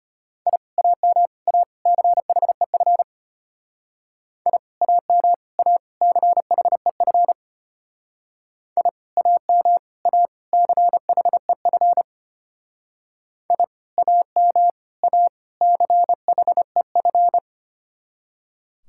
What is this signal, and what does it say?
Morse code, heard as a keyed tone, about 30 wpm, 700 Hz. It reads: I AM A CHEF I AM A CHEF I AM A CHEF I AM A CHEF
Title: Speed-Racing Common QSO Phrases - 25wpm